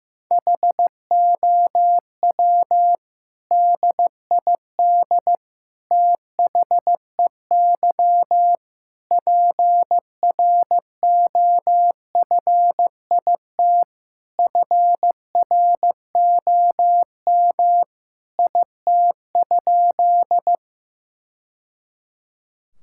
Morse code, heard as a keyed tone, about 15 wpm, 700 Hz. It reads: HOW DID THEY PROFIT FROM IT?
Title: Sentences from Top 1000 Words - Review Part 5 - 15wpm